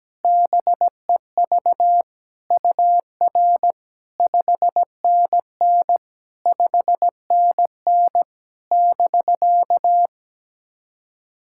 Morse code, heard as a keyed tone, about 17 wpm, 700 Hz. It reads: BEV UR 5NN 5NN <BK>
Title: SOTA QSO Practice 17wpm